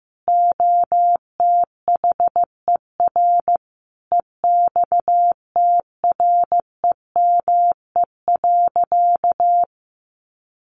Morse code, heard as a keyed tone, about 14 wpm, 700 Hz. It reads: OTHER EXTREME.
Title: Sentences from Top 1300 Words 15wpm